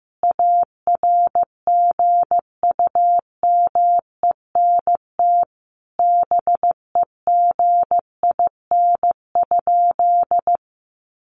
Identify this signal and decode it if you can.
Morse code, heard as a keyed tone, about 15 wpm, 700 Hz. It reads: ARGUMENT BEGIN?
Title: Sentences from Top 1400 Words 15wpm